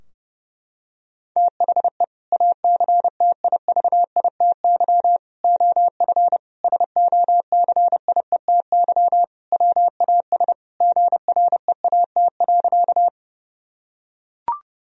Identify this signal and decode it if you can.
Morse code, heard as a keyed tone, about 28 wpm, 700 Hz. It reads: THE ACTIVITY OF SOCIETY WAS GREAT.
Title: Sentences from Top 1000 Words - Review Part 4 - 30wpm